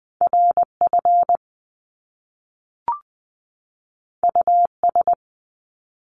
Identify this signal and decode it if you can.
Morse code, heard as a keyed tone, about 20 wpm, 700 Hz. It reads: RF US
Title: Top Two Letter Combinations in Words 20wpm